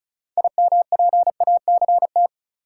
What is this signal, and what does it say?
Morse code, heard as a keyed tone, about 34 wpm, 700 Hz. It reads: IMPACT
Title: Sentences from Top 1000 Words - Review Part 6 - New Word Drill - 35wpm